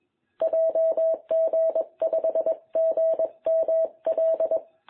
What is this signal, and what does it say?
Morse code, heard as a keyed tone, about 21 wpm, 660 Hz. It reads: JG5GML
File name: jh8cbh / 2021cw